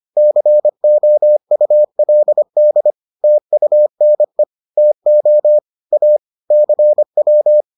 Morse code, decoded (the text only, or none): COULD TUNE TO A CW